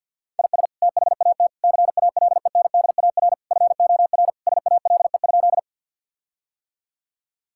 Morse code, heard as a keyed tone, about 47 wpm, 700 Hz. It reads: IS THAT CALENDAR FOR SALE?
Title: Sentences from Top 1600 Words 50wpm